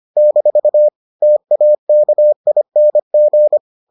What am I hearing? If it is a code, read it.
Morse code, heard as a keyed tone, about 25 wpm, 600 Hz. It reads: - TAKING